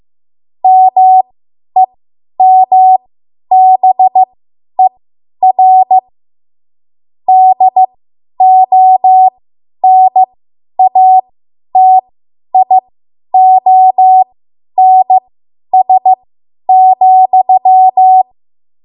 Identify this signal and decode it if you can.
Morse code, heard as a keyed tone, about 15 wpm, spaced as for 10 wpm, 750 Hz. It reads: MEMBER DONATIONS,
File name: Morse / 10WPM